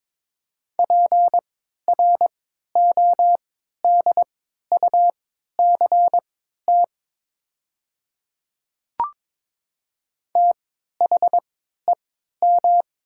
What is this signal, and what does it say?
Morse code, heard as a keyed tone, about 22 wpm, spaced as for 12 wpm, 700 Hz. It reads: PRODUCT THEM
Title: 3x Character Spacing - Top 500 Words - 22wpm